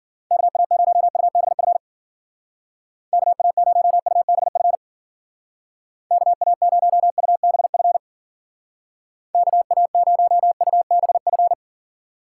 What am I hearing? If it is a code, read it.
Morse code, heard as a keyed tone, about 49 wpm, 700 Hz. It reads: KA0UBF KA0UBF KA0UBF KA0UBF
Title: Speed Racing - USA 2x3 Call Signs 40wpm